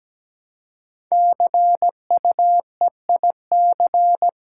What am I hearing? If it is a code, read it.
Morse code, heard as a keyed tone, about 17 wpm, 700 Hz. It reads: CUEIC
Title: Code Groups - 5 Letter:Numbers - 17wpm